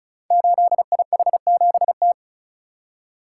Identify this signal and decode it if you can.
Morse code, heard as a keyed tone, about 36 wpm, 700 Hz. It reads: 8IH7T
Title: Code Groups - 5 Letter:Numbers - 35wpm